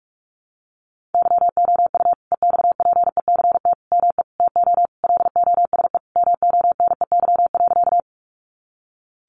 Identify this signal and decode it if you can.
Morse code, heard as a keyed tone, about 44 wpm, 700 Hz. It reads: YOU EXPECT ME TO LOSE MONEY.
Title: Sentences from Top 1100 Words 45wpm